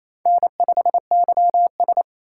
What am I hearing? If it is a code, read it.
Morse code, heard as a keyed tone, about 35 wpm, 700 Hz. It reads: N5YS